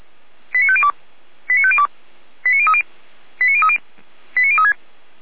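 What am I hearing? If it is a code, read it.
Five-tone selective call (ZVEI), heard as quick runs of five tones, tones of about 70 ms. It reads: 89502 89502 89030 89030 89036
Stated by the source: ZVEI five-tone selective call received on an SDR, logged by Feuerwehr Orlamünde fire brigade.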